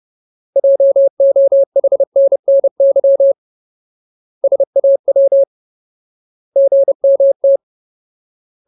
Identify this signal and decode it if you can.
Morse code, heard as a keyed tone, about 30 wpm, 550 Hz. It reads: JOHNNY SAW GMT
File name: Ultimate Word List / 30WPM Ultimate